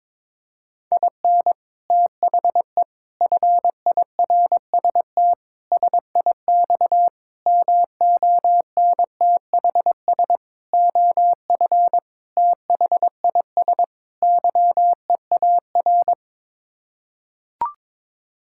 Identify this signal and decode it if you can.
Morse code, heard as a keyed tone, about 22 wpm, 700 Hz. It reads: IN THE FIRST SIX MONTHS OF THIS YEAR
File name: practice / 22wpm / Sets of 8 Words - 22wpm